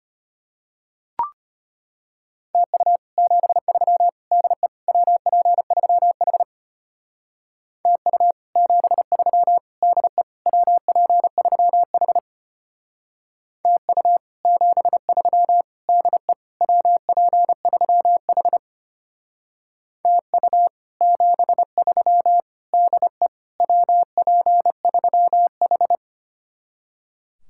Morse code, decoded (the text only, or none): TU 73 DE WP3H TU 73 DE WP3H TU 73 DE WP3H TU 73 DE WP3H